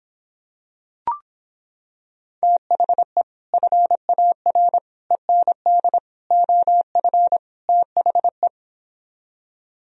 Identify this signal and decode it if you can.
Morse code, heard as a keyed tone, about 26 wpm, 700 Hz. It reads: THE FAR END OF THE